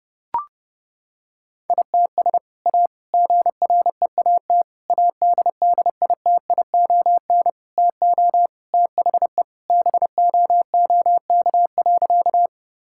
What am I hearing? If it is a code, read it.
Morse code, heard as a keyed tone, about 32 wpm, 700 Hz. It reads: ITS A GREAT ADDITION TO THE BOOK.
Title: Sentences from Top 1000 Words - Review Part 6 - 30wpm